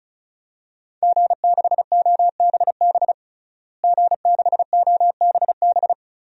Morse code, heard as a keyed tone, about 36 wpm, 700 Hz. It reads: G6OBB G6OBB
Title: DXCC-100 Call Signs - 5 Characters - Repeated 2x - 35wpm